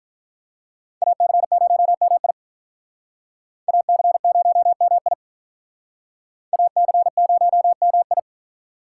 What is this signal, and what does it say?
Morse code, heard as a keyed tone, about 44 wpm, 700 Hz. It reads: AC0MI AC0MI AC0MI